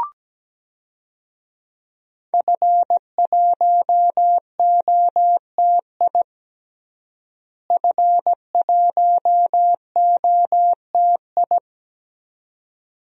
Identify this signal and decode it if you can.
Morse code, heard as a keyed tone, about 17 wpm, 700 Hz. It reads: F1OTI F1OTI
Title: DXCC-100 Call Signs - 5 Characters - Repeated 2x - 17wpm